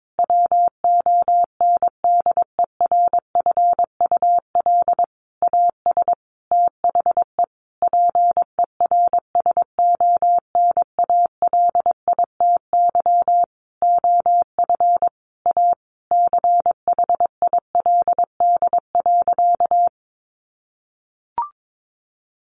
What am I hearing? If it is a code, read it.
Morse code, heard as a keyed tone, about 22 wpm, 700 Hz. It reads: WONDERFUL AS THE PERSONALITY OF A CHILD.
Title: Sentences from Top 1400 Words 22wpm